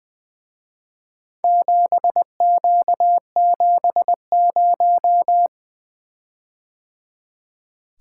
Morse code, heard as a keyed tone, about 23 wpm, 700 Hz. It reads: 7Q70
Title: Speed-Racing - Four Letter-Number Combinations - 20wpm